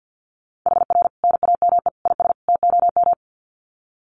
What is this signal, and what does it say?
Morse code, heard as a keyed tone, about 50 wpm, 700 Hz. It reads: HR NAME IS TOM
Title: Common QSO Phrases 50wpm